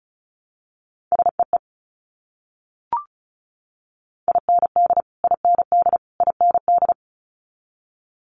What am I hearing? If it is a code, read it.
Morse code, heard as a keyed tone, about 36 wpm, 700 Hz. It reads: SEE IND IND IND
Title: Multi-letter Phonemes Repeated 3x 35wpm